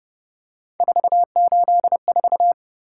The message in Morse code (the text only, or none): 484